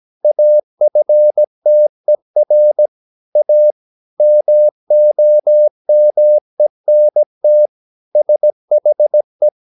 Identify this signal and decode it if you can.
Morse code, heard as a keyed tone, about 17 wpm, 600 Hz. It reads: AFTER A MOMENT SHE
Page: Romanen Never af Ken Follet på engelsk.